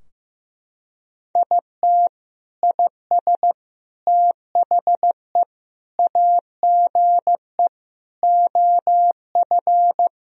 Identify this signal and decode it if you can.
Morse code, heard as a keyed tone, about 15 wpm, 700 Hz. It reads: IT IS THE AGE OF
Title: Sentences from Top 1000 Words - Review Part 3 - 15wpm